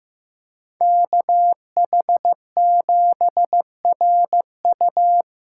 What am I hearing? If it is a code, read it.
Morse code, heard as a keyed tone, about 15 wpm, 700 Hz. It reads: KH7RU
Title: USA Callsign DE Callsign - 5 Characters - 15wpm